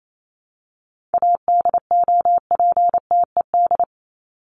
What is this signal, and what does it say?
Morse code, heard as a keyed tone, about 28 wpm, 700 Hz. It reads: ADOPTED
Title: ED Suffix - Top 100 Words with Base Words 28wpm